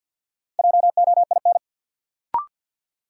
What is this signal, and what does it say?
Morse code, heard as a keyed tone, about 51 wpm, 700 Hz. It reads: JOIN
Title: Sentences from Top 1400 Words - New Word Drill Alt - 50wpm